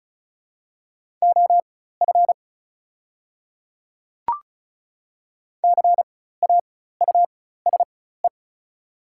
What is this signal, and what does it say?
Morse code, heard as a keyed tone, about 34 wpm, spaced as for 16 wpm, 700 Hz. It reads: OF CAUSE
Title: 4x Character Spacing - Top 300 Words - 35wpm